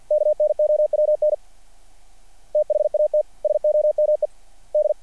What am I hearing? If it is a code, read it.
Morse code, heard as a keyed tone, about 49 wpm, 600 Hz. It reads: KNOWN THAT SOME D
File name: archived / CRSnet